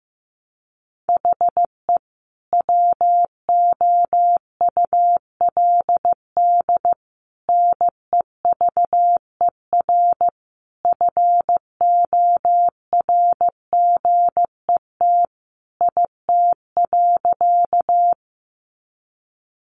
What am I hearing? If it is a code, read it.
Morse code, heard as a keyed tone, about 15 wpm, 700 Hz. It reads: HE WOULD NEVER FORGET IT.
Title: Sentences from Top 1200 Words 15wpm